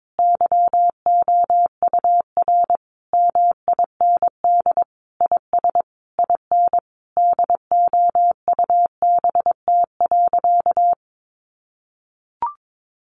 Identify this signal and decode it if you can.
Morse code, heard as a keyed tone, about 22 wpm, 700 Hz. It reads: YOUR MIND IS IN DOUBT.
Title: Sentences from Top 1500 Words 22wpm